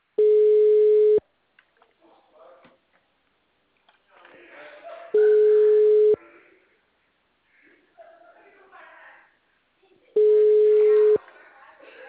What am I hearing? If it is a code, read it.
Ringback tone (Europe, 425 Hz), with speech.